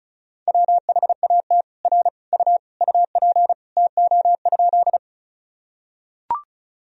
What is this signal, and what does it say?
Morse code, heard as a keyed tone, about 36 wpm, 700 Hz. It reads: WHAT R U UP TO?
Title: Sentences from Top 100 Words and Abbreviations 35wpm